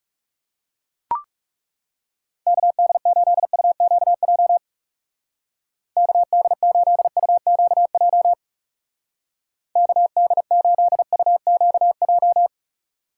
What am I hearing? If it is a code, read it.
Morse code, heard as a keyed tone, about 35 wpm, 700 Hz. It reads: KD8UQJ KD8UQJ KD8UQJ
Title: Speed Racing - USA 2x3 Call Signs 30wpm